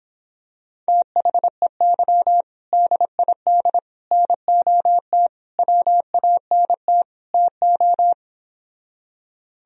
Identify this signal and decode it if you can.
Morse code, heard as a keyed tone, about 26 wpm, 700 Hz. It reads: THEY DID NOT WANT TO